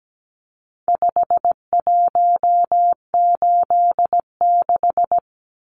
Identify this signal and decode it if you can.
Morse code, heard as a keyed tone, about 17 wpm, 700 Hz. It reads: 5186
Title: Numbers - 4-Digit 17wpm